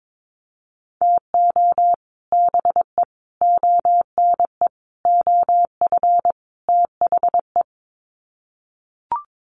Morse code, heard as a keyed tone, about 22 wpm, 700 Hz. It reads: TO BE ONE OF THE
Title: Sets of 5 Words 22wpm